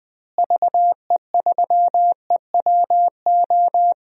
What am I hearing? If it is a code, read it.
Morse code, heard as a keyed tone, about 21 wpm, 700 Hz. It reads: VE3EWO
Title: DXCC-100 Call Signs - 6 Characters - Repeated 3x - 20wpm